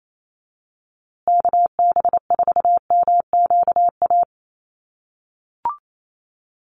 Morse code, heard as a keyed tone, about 28 wpm, 700 Hz. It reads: KB4MQA
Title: USA Call Signs Repeated 3x 28wpm